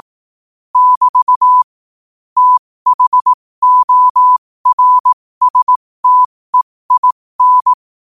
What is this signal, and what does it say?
Morse code, heard as a keyed tone, about 18 wpm, spaced as for 15 wpm, 1000 Hz. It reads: <BT> THORSTEIN
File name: qotdfiles / 15wpm